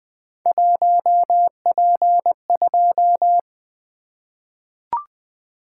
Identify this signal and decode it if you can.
Morse code, heard as a keyed tone, about 20 wpm, 700 Hz. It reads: 1P2
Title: Three Letter-Number Combinations - 20wpm